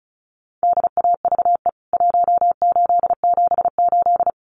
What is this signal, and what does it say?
Morse code, heard as a keyed tone, about 34 wpm, 700 Hz. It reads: DAVE 1878
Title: CWT Exchange 35wpm